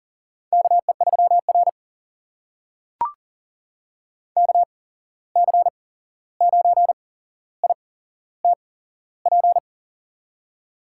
Morse code, heard as a keyed tone, about 40 wpm, spaced as for 15 wpm, 700 Hz. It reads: KE3R KC9ITP
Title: USA Call Signs - 8x Character Spacing - 40wpm